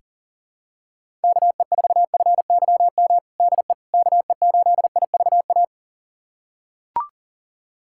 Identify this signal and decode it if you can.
Morse code, heard as a keyed tone, about 40 wpm, 700 Hz. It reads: KE4FYM DE KE8IVA
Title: USA Callsign DE Callsign - 6 Characters - 40wpm